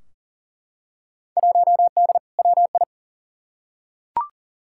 Morse code, heard as a keyed tone, about 40 wpm, 700 Hz. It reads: 1D WI
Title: Field Day Exchanges - 40wpm